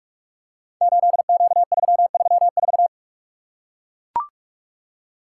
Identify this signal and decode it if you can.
Morse code, heard as a keyed tone, about 44 wpm, 700 Hz. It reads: 8Q334
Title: Code Groups - 5 Letter:Numbers - 45wpm